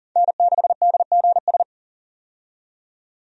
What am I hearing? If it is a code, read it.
Morse code, heard as a keyed tone, about 40 wpm, 700 Hz. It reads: N6DGS
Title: USA Call Signs - 5 Characters - 40wpm